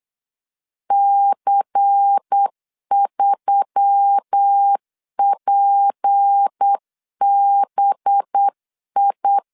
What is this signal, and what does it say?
Morse code, heard as a keyed tone, about 8 wpm, 800 Hz. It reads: C3PBI
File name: LEZIONI / 7 GF9Z / 7 GF 8wpm 3108